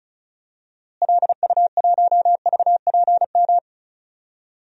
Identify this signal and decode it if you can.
Morse code, heard as a keyed tone, about 36 wpm, 700 Hz. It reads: LU1VPM
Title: DXCC-100 Call Signs - 6 Characters - Repeated 2x - 35wpm